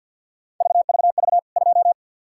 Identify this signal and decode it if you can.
Morse code, heard as a keyed tone, about 52 wpm, 700 Hz. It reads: VVV 2
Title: K1USN SST QSO Practice 50wpm